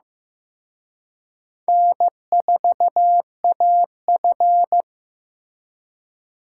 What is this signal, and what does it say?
Morse code, heard as a keyed tone, about 18 wpm, 700 Hz. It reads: N4AF
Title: Speed Racing - USA 1x2 Call Signs 15wpm